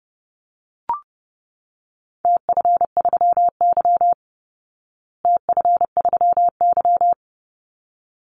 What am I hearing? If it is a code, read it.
Morse code, heard as a keyed tone, about 30 wpm, 700 Hz. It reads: TF3Y TF3Y
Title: Call Signs Repeated 3x - 30wpm